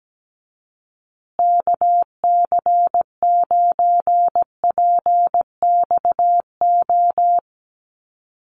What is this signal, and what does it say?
Morse code, heard as a keyed tone, about 17 wpm, 700 Hz. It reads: KC9PXO
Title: USA Call Signs Repeated 3x 17wpm